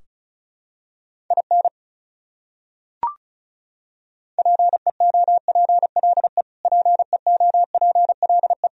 Morse code, heard as a keyed tone, about 36 wpm, 700 Hz. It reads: IN PEOPLE PEOPLE
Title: Top 100 Words Repeated 35wpm